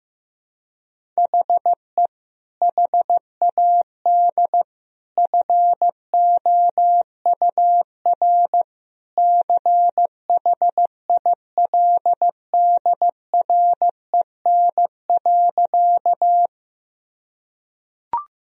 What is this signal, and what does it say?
Morse code, heard as a keyed tone, about 15 wpm, 700 Hz. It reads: HE HAD FOUR CHILDREN.
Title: Sentences from Top 1000 Words - Review Part 1 - 15wpm